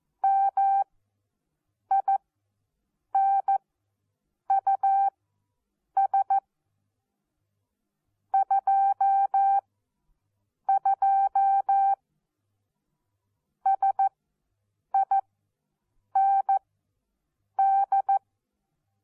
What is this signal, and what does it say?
Morse code, heard as a keyed tone, about 14 wpm, spaced as for 7 wpm, 780 Hz. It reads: MINUS 22 SIND
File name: Onlinelehrgang / morsen / DJ4UF Morsen14